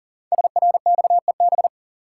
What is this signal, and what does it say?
Morse code, heard as a keyed tone, about 40 wpm, 700 Hz. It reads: SRXEB
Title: Code Groups - 5 Letters - 40wpm